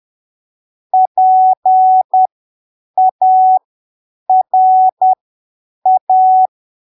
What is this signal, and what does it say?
Morse code, heard as a keyed tone, about 10 wpm, spaced as for 7 wpm, 750 Hz. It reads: PARA
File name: arquivos / texto / texto 24 10ppm